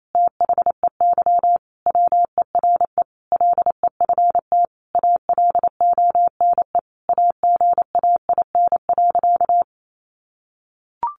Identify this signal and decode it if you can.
Morse code, heard as a keyed tone, about 28 wpm, 700 Hz. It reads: THEY WERE LEFT ALONE AGAIN.
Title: Sentences from Top 1000 Words 28wpm